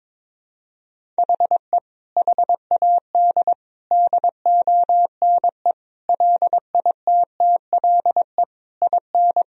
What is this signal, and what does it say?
Morse code, heard as a keyed tone, about 22 wpm, 700 Hz. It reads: HE HAD DONE LITTLE IN
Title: Sentences from Top 1000 Words - Review Part 3 - 22wpm